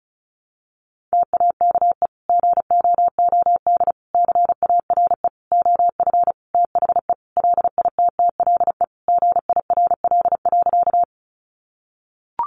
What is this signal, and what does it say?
Morse code, heard as a keyed tone, about 34 wpm, 700 Hz. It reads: TAKE GOOD CARE OF THE LITTLE GIRL.
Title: Sentences from Top 400 Words 35wpm